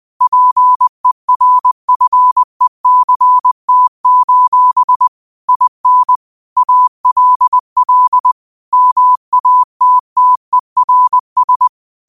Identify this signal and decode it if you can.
Morse code, heard as a keyed tone, about 20 wpm, 1000 Hz. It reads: PERFECT: IN ALL MATTERS